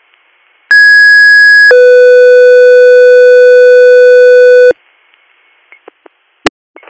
Two-tone page, heard as a one-second tone, then a three-second tone, about 1650 then 500 Hz. It is heard from 0.7 to 4.7 s.